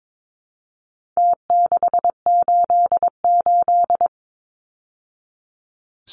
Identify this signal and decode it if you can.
Morse code, heard as a keyed tone, about 22 wpm, 700 Hz. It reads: T688
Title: Numbers - Contest Style - 4-Digit 22wpm